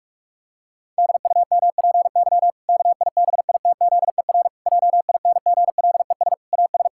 Morse code, heard as a keyed tone, about 46 wpm, 700 Hz. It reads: DUMPY KIBITZER JINGLES AS
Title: Pangram Challenge v2 - 45wpm